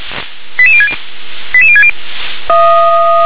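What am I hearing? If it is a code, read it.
Five-tone selective call (ZVEI), heard as quick runs of five tones, tones of about 70 ms. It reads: 88060 88060
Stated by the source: ZVEI five-tone selective call received by SDR.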